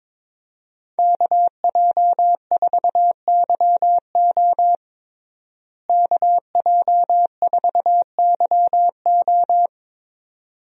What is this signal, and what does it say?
Morse code, heard as a keyed tone, about 21 wpm, 700 Hz. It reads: KJ4YO KJ4YO
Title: USA Call Signs - 5 Characters - Repeated 2x - 22wpm